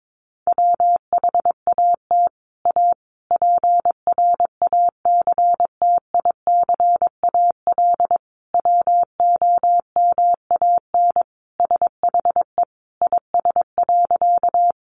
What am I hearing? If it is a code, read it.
Morse code, heard as a keyed tone, about 22 wpm, 700 Hz. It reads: WHAT A PRACTICAL WOMAN SHE IS.
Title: Sentences from Top 1400 Words 22wpm